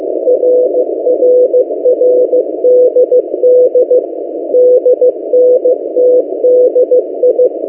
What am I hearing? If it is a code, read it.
Morse code, heard as a keyed tone, about 15 wpm, 500 Hz. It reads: RRRDD DNTDI
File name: M08a- -1355utc-8096khz